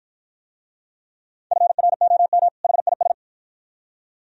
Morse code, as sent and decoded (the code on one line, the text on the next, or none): ..-. .-. --- -- / .... .. ...
FROM HIS